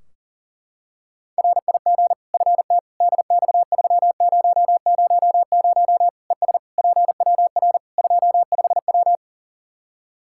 Morse code, heard as a keyed tone, about 37 wpm, 700 Hz. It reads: RIG FT DX3000 ES PWR 25W
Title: Practice QSO - Exchange 2 - Info - 40wpm